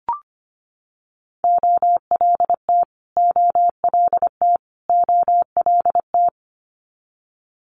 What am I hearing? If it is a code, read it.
Morse code, heard as a keyed tone, about 25 wpm, 700 Hz. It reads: OLT OLT OLT
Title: Multi-letter Phonemes Repeated 3x 25wpm